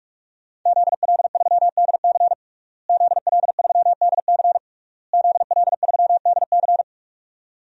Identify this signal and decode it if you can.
Morse code, heard as a keyed tone, about 43 wpm, 700 Hz. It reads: ZL3DC ZL3DC ZL3DC
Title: DXCC-100 Call Signs - 5 Characters - Repeated 3x - 45wpm